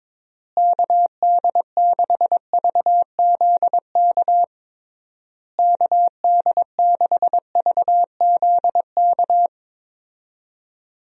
Morse code, heard as a keyed tone, about 22 wpm, 700 Hz. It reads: KD6VZK KD6VZK
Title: USA Call Signs - 6 Characters - Repeated 3x - 22wpm